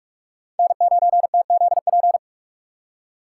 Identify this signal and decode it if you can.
Morse code, heard as a keyed tone, about 44 wpm, 700 Hz. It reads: N9TZP